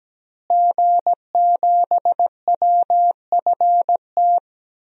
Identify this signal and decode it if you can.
Morse code, heard as a keyed tone, about 17 wpm, 700 Hz. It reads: G7WFT
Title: DXCC-100 Call Signs - 5 Characters - Repeated 2x - 17wpm